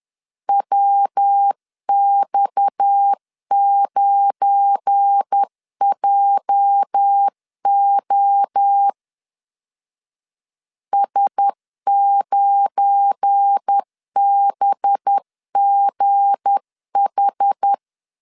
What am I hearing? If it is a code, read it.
Morse code, heard as a keyed tone, about 10 wpm, 800 Hz. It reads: WX9JO S9BGH
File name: LEZIONI / 9 28 10wpm 2842